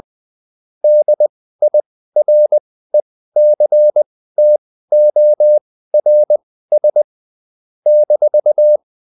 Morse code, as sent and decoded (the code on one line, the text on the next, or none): -.. .. .-. -.-. - --- .-. ... / -....-
DIRECTORS -